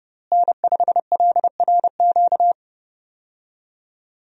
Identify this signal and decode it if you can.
Morse code, heard as a keyed tone, about 29 wpm, 700 Hz. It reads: N5LRQ
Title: USA Call Signs - 5 Characters - Repeated 2x - 30wpm